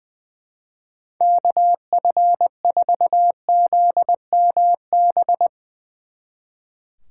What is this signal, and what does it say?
Morse code, heard as a keyed tone, about 22 wpm, 700 Hz. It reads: KF4ZMB
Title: Speed Racing - USA 2x3 Call Signs 20wpm